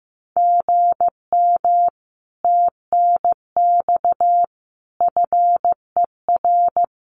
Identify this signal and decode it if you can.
Morse code, heard as a keyed tone, about 15 wpm, 700 Hz. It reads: GM TNX FER
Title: Common QSO Phrases Encore 15wpm